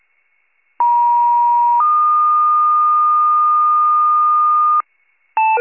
Two-tone page, heard as a one-second tone, then a three-second tone, about 950 then 1220 Hz, with noise. It is heard from 0.8 to 4.8 s.